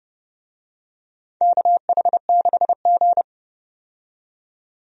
Morse code, heard as a keyed tone, about 30 wpm, 700 Hz. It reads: KH6G
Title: DXCC-100 Call Signs - 4 Characters - 30wpm